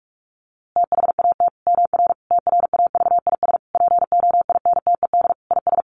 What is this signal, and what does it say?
Morse code, heard as a keyed tone, about 44 wpm, 700 Hz. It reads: THAT MR TRAVIS POINTED IS
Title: A Sound Of Thunder 45wpm